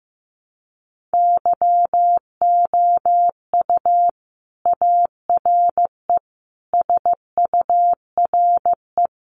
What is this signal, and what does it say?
Morse code, heard as a keyed tone, about 15 wpm, 700 Hz. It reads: YOU ARE SURE
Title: Sentences from Top 300 Words 15wpm